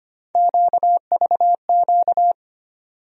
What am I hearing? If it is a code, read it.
Morse code, heard as a keyed tone, about 25 wpm, 700 Hz. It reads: QVQ
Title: USA Call Sign Postfixes - 3 Letters 25wpm